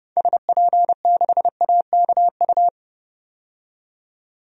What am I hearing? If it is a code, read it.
Morse code, heard as a keyed tone, about 31 wpm, 700 Hz. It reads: SP6AKU
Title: DXCC-100 Call Signs - 6 Characters - 30wpm